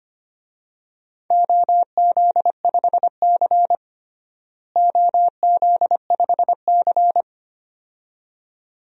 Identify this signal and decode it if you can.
Morse code, heard as a keyed tone, about 25 wpm, 700 Hz. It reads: OZ5C OZ5C
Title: DXCC-100 Call Signs - 4 Characters - Repeated 2x - 25wpm